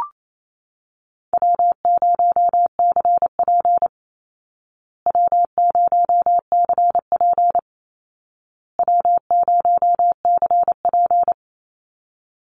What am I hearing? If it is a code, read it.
Morse code, heard as a keyed tone, about 28 wpm, 700 Hz. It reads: W0CP W0CP W0CP
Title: Easy Call Signs Repeated 3x - 28wpm